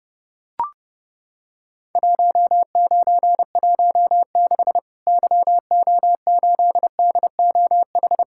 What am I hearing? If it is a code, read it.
Morse code, heard as a keyed tone, about 30 wpm, 700 Hz. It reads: 1916 YO8DOH
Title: Worked All Europe DX Contest - QTC Exchange - 30wpm